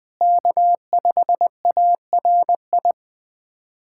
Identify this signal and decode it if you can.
Morse code, heard as a keyed tone, about 20 wpm, 700 Hz. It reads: K5ARI
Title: USA Callsign DE Callsign 20wpm